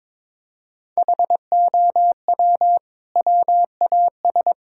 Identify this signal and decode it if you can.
Morse code, heard as a keyed tone, about 22 wpm, 700 Hz. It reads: HOW WAS
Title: Sentences from Top 1600 Words 22wpm